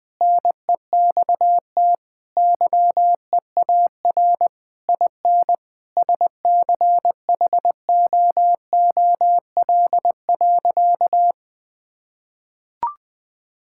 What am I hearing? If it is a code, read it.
Morse code, heard as a keyed tone, about 20 wpm, 700 Hz. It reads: NEXT YEAR IN SCHOOL.
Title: Sentences from Top 1000 Words - Review Part 5 - 20wpm